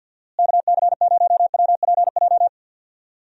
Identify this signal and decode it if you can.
Morse code, heard as a keyed tone, about 52 wpm, 700 Hz. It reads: KC0WPJ
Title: USA Call Signs - 6 Characters - Repeated 2x - 50wpm